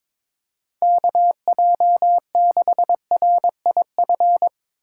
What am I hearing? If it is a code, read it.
Morse code, heard as a keyed tone, about 22 wpm, 700 Hz. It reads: KJ6RIF
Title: USA Call Signs - 6 Characters - 3x Character Spacing - 22wpm